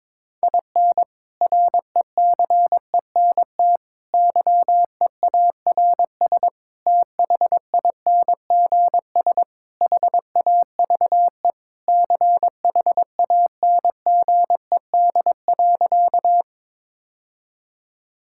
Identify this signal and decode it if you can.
Morse code, heard as a keyed tone, about 21 wpm, 700 Hz. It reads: IN RECENT YEARS THINGS HAVE CHANGED.
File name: practice / 22wpm / Sentences with 3-Word Phrases - 22wpm